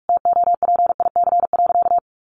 Morse code, heard as a keyed tone, about 44 wpm, 700 Hz. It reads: TOPIC.
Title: Sentences from Top 1200 Words 45wpm